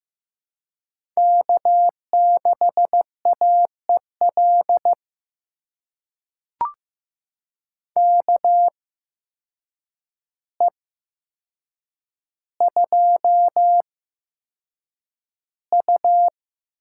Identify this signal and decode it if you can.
Morse code, heard as a keyed tone, about 15 wpm, spaced as for 7 wpm, 700 Hz. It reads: K6AEL KE2U
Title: USA Call Signs - 5 Characters - 8x Character Spacing - 15wpm